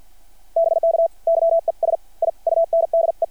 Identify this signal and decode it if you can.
Morse code, heard as a keyed tone, about 53 wpm, 660 Hz. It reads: BK YES I UNDE
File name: archived / CRSnet